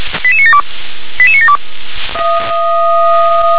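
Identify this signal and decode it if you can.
Five-tone selective call (ZVEI), heard as quick runs of five tones, tones of about 70 ms. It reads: 88972 88972
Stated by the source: ZVEI five-tone selective call received on SDR.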